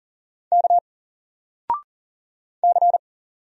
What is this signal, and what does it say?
Morse code, heard as a keyed tone, about 39 wpm, 700 Hz. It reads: K C
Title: Single Character (Letter:Number:Punctuation:Prosign) 40wpm